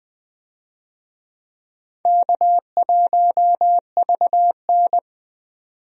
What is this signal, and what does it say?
Morse code, heard as a keyed tone, about 20 wpm, 700 Hz. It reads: K1VN
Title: USA Call Signs - 4 Characters - 20wpm